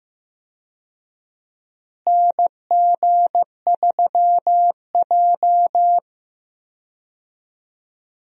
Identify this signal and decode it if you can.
Morse code, heard as a keyed tone, about 15 wpm, 700 Hz. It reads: NG3J
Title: USA Call Signs - 4 Characters - Repeated 2x - 15wpm